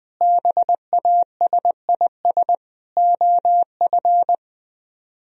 Morse code, heard as a keyed tone, about 20 wpm, 700 Hz. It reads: BASIS OF
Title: Speed-Racing - Sets of 4 Words 20wpm